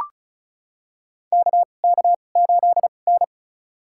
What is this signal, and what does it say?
Morse code, heard as a keyed tone, about 36 wpm, spaced as for 25 wpm, 700 Hz. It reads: KK8N
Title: USA Call Signs - 4 Characters - 2x Character Spacing - 35wpm